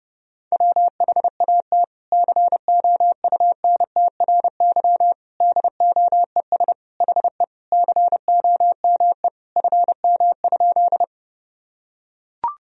Morse code, heard as a keyed tone, about 30 wpm, 700 Hz. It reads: WHAT COUNTRY DOES HE COME FM?
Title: Sentences from Top 300 Words and Abbreviations 30wpm